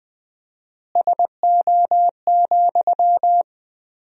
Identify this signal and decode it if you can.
Morse code, heard as a keyed tone, about 20 wpm, 700 Hz. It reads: SO,
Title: 4x Word-Spacing - Sentences from Top 200 Words - 20wpm